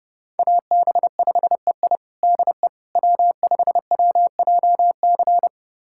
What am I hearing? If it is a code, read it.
Morse code, heard as a keyed tone, about 32 wpm, 700 Hz. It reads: AB5EI DE W5WJC
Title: USA Callsign DE Callsign - 5 Characters - 30wpm